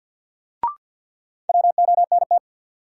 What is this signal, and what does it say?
Morse code, heard as a keyed tone, about 48 wpm, 700 Hz. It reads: WONT
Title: Sentences from Top 1000 Words - Review Part 6 - New Word Drill - 50wpm